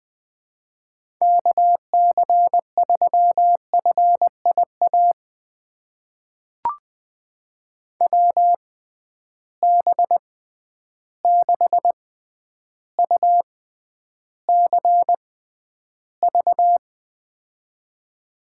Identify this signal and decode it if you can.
Morse code, heard as a keyed tone, about 20 wpm, spaced as for 10 wpm, 700 Hz. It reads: KC3FIA WB6UCV
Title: USA Call Signs - 6 Characters - 6x Character Spacing - 20wpm